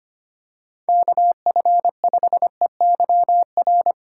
Morse code, heard as a keyed tone, about 28 wpm, 700 Hz. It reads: KF5EYR
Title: Speed Racing - USA 2x3 Call Signs 25wpm